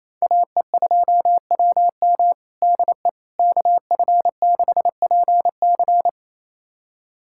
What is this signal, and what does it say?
Morse code, heard as a keyed tone, about 28 wpm, 700 Hz. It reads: AE2WM DE KF6PC
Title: USA Callsign DE Callsign - 5 Characters - 28wpm